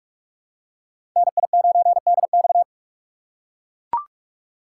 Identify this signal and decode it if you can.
Morse code, heard as a keyed tone, about 42 wpm, 700 Hz. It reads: NI9DX
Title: USA Call Signs - 5 Characters - 3x Character Spacing - 45wpm